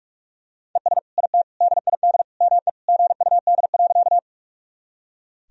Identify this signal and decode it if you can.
Morse code, heard as a keyed tone, about 46 wpm, 700 Hz. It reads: ES IT DID ME GUD.
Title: Sentences from Top 200 Words and Abbreviations 45wpm